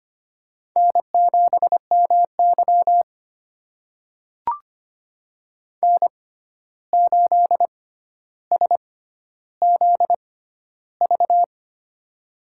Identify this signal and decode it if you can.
Morse code, heard as a keyed tone, about 25 wpm, spaced as for 12 wpm, 700 Hz. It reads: N7MY N8SZV
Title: USA Call Signs - 6x Character Spacing - 25wpm